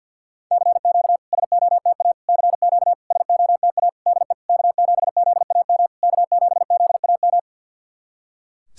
Morse code, heard as a keyed tone, about 50 wpm, 700 Hz. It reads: CQ SOTA CQ SOTA DE K7ZAM K7ZAM
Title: SOTA QSO Practice 50wpm